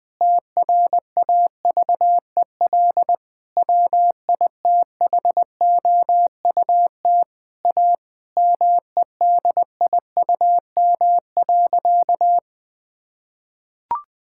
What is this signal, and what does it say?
Morse code, heard as a keyed tone, about 20 wpm, 700 Hz. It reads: TRAVEL WITHOUT A MEDIUM.
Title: Sentences from Top 1000 Words - Review Part 6 - 20wpm